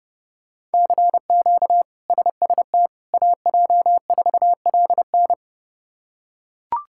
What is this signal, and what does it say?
Morse code, heard as a keyed tone, about 30 wpm, 700 Hz. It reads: CQ SST AJ4LN
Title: K1USN SST QSO Practice - 30wpm